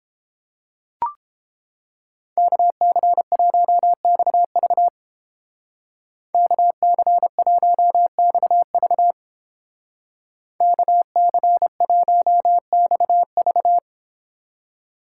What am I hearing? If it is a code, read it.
Morse code, heard as a keyed tone, about 30 wpm, 700 Hz. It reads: KC1XV KC1XV KC1XV